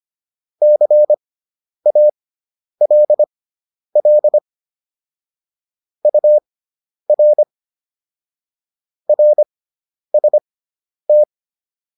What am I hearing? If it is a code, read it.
Morse code, heard as a keyed tone, about 25 wpm, spaced as for 10 wpm, 600 Hz. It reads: CALL UR RST